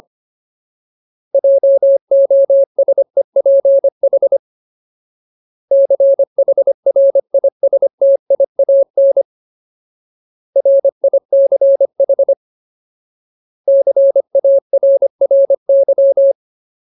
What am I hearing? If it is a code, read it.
Morse code, heard as a keyed tone, about 25 wpm, 550 Hz. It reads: JOSEPH CHRISTIAN RICH CARRY
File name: Ultimate Word List / 25WPM Ultimate